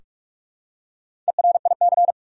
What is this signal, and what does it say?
Morse code, heard as a keyed tone, about 42 wpm, 700 Hz. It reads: ERIC
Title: USA Common Names - Max 4 Letters - 45wpm